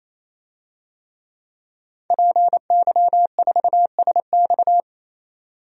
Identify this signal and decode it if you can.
Morse code, heard as a keyed tone, about 28 wpm, 700 Hz. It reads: PY4SX
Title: DXCC-100 Call Signs - 5 Characters - 28wpm